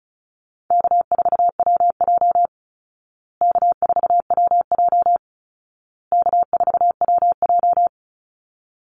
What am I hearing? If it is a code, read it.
Morse code, heard as a keyed tone, about 34 wpm, 700 Hz. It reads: K4WJ K4WJ K4WJ
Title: Call Signs Repeated 2x - 35wpm